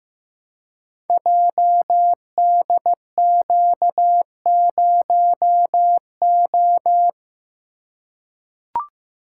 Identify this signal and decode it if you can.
Morse code, heard as a keyed tone, about 15 wpm, 700 Hz. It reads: JDQ0O
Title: Code Groups - 5 Letter:Numbers - 15wpm